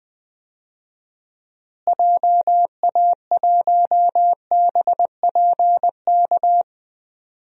Morse code, heard as a keyed tone, about 20 wpm, 700 Hz. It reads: JA1BPK
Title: DXCC-100 Call Signs - 6 Characters - Repeated 3x - 20wpm